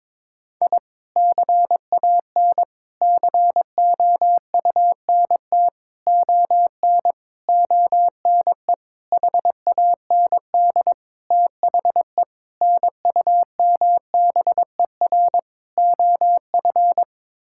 Morse code, heard as a keyed tone, about 21 wpm, 700 Hz. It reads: I CAN COUNT ON ONE HAND THE NUMBER OF
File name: Sets of 9 Words - 22wpm